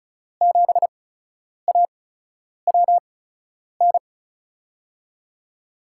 Morse code, heard as a keyed tone, about 34 wpm, spaced as for 10 wpm, 700 Hz. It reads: 7AWN
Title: USA Call Signs - 6 Characters - 8x Character Spacing - 35wpm